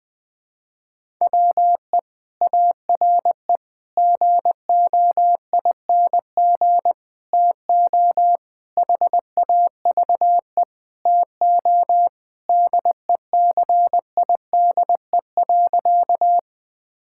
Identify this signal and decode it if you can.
Morse code, heard as a keyed tone, about 19 wpm, 700 Hz. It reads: WE ARE GOING TO HAVE TO DECIDE.
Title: Sentences with 5-Word Phrases - 20wpm